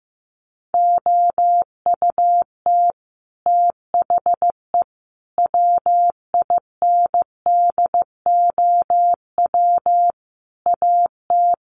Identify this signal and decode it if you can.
Morse code, heard as a keyed tone, about 15 wpm, 700 Hz. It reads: OUT THE WINDOW AT
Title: Sets of 5 Words Encore2 15wpm